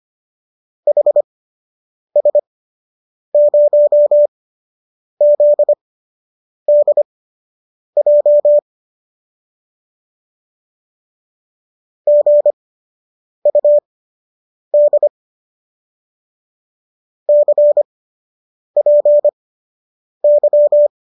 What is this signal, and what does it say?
Morse code, heard as a keyed tone, about 25 wpm, spaced as for 8 wpm, 590 Hz. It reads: HS0ZDJ GUD CPY